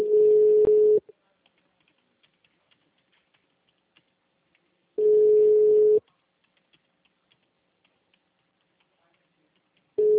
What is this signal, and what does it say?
Ringback tone (Europe, 425 Hz), with speech.